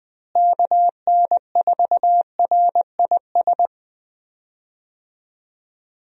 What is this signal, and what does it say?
Morse code, heard as a keyed tone, about 21 wpm, 700 Hz. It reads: KN4RIS
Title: USA Call Signs - 6 Characters - Repeated 3x - 20wpm